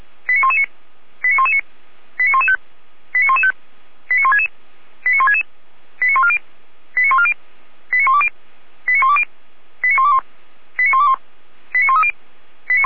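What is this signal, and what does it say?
Five-tone selective call (ZVEI), heard as quick runs of five tones, tones of about 70 ms. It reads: 89119 89119 89115 89115 89166 89166 89140 89140 89120 89120 89121 89121 89130
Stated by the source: Received by SDR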